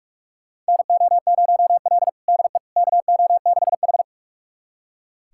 Morse code, heard as a keyed tone, about 44 wpm, 700 Hz. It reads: NO0L DE KO6H
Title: USA Callsign DE Callsign - 4 Characters - 45wpm